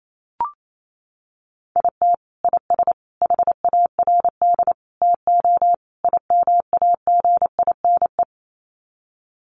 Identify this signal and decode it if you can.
Morse code, heard as a keyed tone, about 28 wpm, 700 Hz. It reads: IT IS HARD TO IMAGINE
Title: Sets of 5 Words Encore2 28wpm